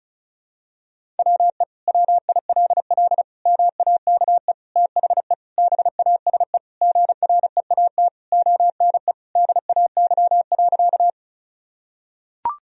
Morse code, heard as a keyed tone, about 36 wpm, 700 Hz. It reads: WE WILL MAKE THE BASE GREAT ONE DAY.
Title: Sentences from Top 1000 Words - Review Part 1 - 35wpm